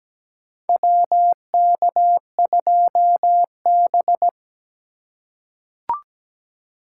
Morse code, heard as a keyed tone, about 17 wpm, 700 Hz. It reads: WK2B
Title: USA Call Signs - 4 Characters - 4x Character Spacing - 17wpm